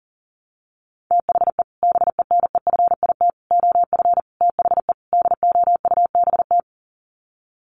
Morse code, heard as a keyed tone, about 40 wpm, 700 Hz. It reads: THE BENEFIT OF THE DOUBT
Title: Sets of 5 Words Encore 40wpm